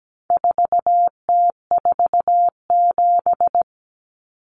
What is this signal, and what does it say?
Morse code, heard as a keyed tone, about 17 wpm, 700 Hz. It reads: 4T47
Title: Common Cut Numbers - 4-Digit 17wpm